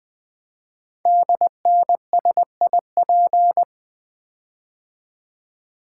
Morse code, heard as a keyed tone, about 20 wpm, 700 Hz. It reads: DNSIP
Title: Code Groups - 5 Letters - 20wpm